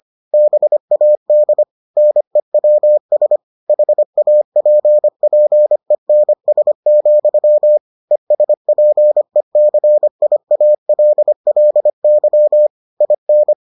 Morse code, heard as a keyed tone, about 25 wpm, 600 Hz. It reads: BAD NEWS HAPPENS, ESPECIALLY IN